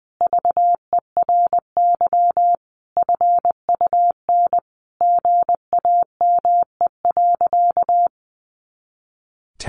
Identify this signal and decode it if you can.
Morse code, heard as a keyed tone, about 20 wpm, 700 Hz. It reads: VERY FUN GAME.
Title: Additional Easy Phrases 20wpm